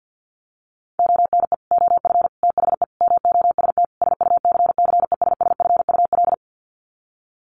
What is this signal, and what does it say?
Morse code, heard as a keyed tone, about 50 wpm, 700 Hz. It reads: ONE OF THE MOST SUCCESSFUL
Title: Sets of 5 Words Encore2 50wpm